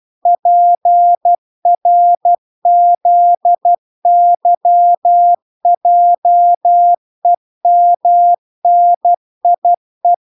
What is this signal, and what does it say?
Morse code, heard as a keyed tone, about 12 wpm, 700 Hz. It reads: PRZYJEMNIE
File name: recordings / 12wpm0000